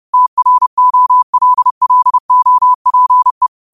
Morse code, heard as a keyed tone, about 30 wpm, 1000 Hz. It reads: TROLLOPE
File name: qotdfiles / 30wpm